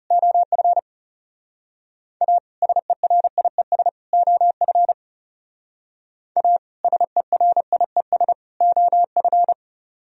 Morse code, heard as a keyed tone, about 38 wpm, 700 Hz. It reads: OF A SERIES OF A SERIES OF